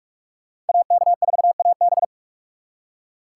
Morse code, heard as a keyed tone, about 59 wpm, 700 Hz. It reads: AK4AB